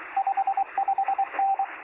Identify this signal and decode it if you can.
Morse code, heard as a keyed tone, about 24 wpm, 780 Hz, with radio noise. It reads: 55N
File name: sm7lcb / 0319z yl2ao 23cm